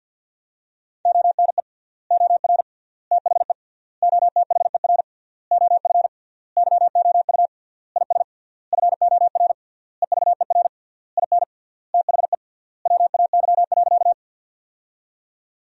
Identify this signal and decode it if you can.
Morse code, heard as a keyed tone, about 50 wpm, 700 Hz. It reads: ONE OR THE OTHER OF YOU IS FOR EVER IN THE WAY.
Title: 3x Word-Spacing - Sentences from Top 400 Words - 50wpm